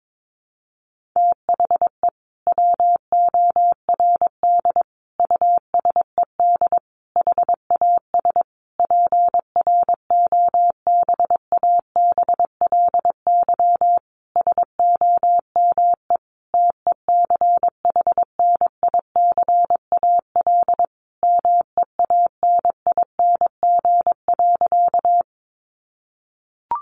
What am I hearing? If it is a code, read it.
Morse code, heard as a keyed tone, about 22 wpm, 700 Hz. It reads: THE WORD USED HAS PROBABLY SOME TECHNICAL MEANING.
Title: Sentences from Top 1100 Words 22wpm